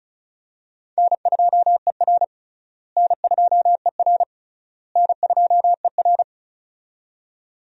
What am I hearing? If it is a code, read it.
Morse code, heard as a keyed tone, about 36 wpm, 700 Hz. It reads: N2ER N2ER N2ER
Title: USA Call Signs - 4 Characters - Repeated 3x - 35wpm